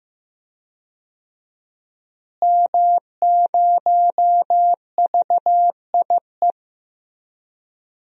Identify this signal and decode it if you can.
Morse code, heard as a keyed tone, about 15 wpm, 700 Hz. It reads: M0VIE
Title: DXCC-100 Call Signs - 5 Characters - Repeated 2x - 15wpm